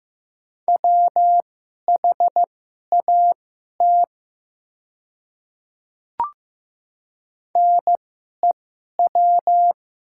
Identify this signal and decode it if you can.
Morse code, heard as a keyed tone, about 15 wpm, spaced as for 11 wpm, 700 Hz. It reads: WHAT NEW
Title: 2x Character Spacing - Top 300 Words - 15wpm